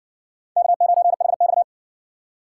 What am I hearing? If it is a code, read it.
Morse code, heard as a keyed tone, about 55 wpm, 700 Hz. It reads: K9HX